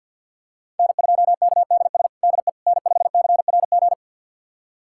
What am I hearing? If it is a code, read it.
Morse code, heard as a keyed tone, about 52 wpm, 700 Hz. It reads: N2KDS DE N5CRG
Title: USA Callsign DE Callsign - 5 Characters - 50wpm